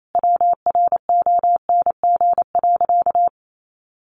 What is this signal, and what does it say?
Morse code, heard as a keyed tone, about 28 wpm, 700 Hz. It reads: WRONG.
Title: Sentences from Top 900 Words 28wpm